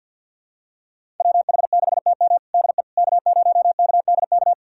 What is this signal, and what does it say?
Morse code, heard as a keyed tone, about 46 wpm, 700 Hz. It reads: WH6TM DE K0KDK
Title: USA Callsign DE Callsign - 5 Characters - 50wpm